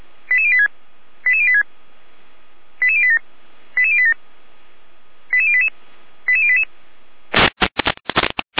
Five-tone selective call (ZVEI), heard as quick runs of five tones, tones of about 70 ms. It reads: 88086 88086 88087 88087 88088 88088